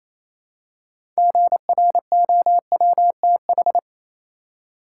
Morse code, heard as a keyed tone, about 28 wpm, 700 Hz. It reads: GROWTH
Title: Six Letter Words - 28wpm